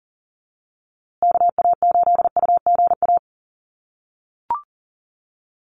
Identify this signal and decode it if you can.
Morse code, heard as a keyed tone, about 40 wpm, 700 Hz. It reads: KA8UGA
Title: USA Call Signs Repeated 2x 40wpm